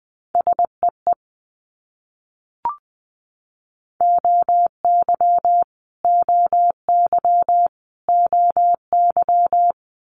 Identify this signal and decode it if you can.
Morse code, heard as a keyed tone, about 20 wpm, 700 Hz. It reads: SEE OY OY OY
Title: Multi-letter Phonemes Repeated 3x 20wpm